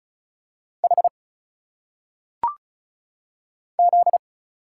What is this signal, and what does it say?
Morse code, heard as a keyed tone, about 36 wpm, 700 Hz. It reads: H Z
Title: Single Letters - Phonetic Pronunciation 35wpm